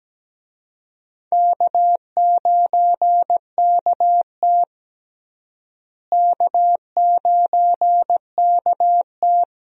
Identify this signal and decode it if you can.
Morse code, heard as a keyed tone, about 17 wpm, 700 Hz. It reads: K9KT K9KT
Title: USA Call Signs - 4 Characters - Repeated 2x - 17wpm